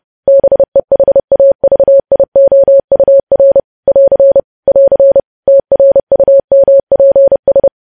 Morse code, heard as a keyed tone, about 30 wpm, 550 Hz. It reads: BEHAVIOUR <AR> <AR> TRUMPS
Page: news headlines in morse code at 30 wpm